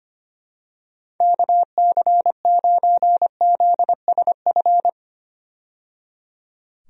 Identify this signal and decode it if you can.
Morse code, heard as a keyed tone, about 25 wpm, 700 Hz. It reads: KC9ZSF
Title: USA Call Signs - 6 Characters - 25wpm